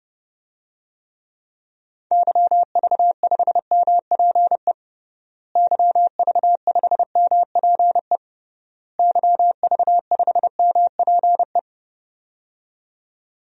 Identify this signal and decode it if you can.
Morse code, heard as a keyed tone, about 30 wpm, 700 Hz. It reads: YV5MPE YV5MPE YV5MPE
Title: DXCC-100 Call Signs - 6 Characters - Repeated 3x - 30wpm